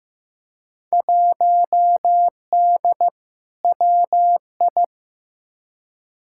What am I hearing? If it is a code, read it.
Morse code, heard as a keyed tone, about 15 wpm, 700 Hz. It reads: 1D WI
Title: Field Day Exchanges - 15wpm